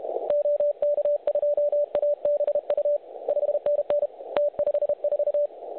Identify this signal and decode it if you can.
Morse code, heard as a keyed tone, about 30 wpm, 590 Hz. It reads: OK2ABU 5NN T54
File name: truSDX / websdr recording 3541.9kHz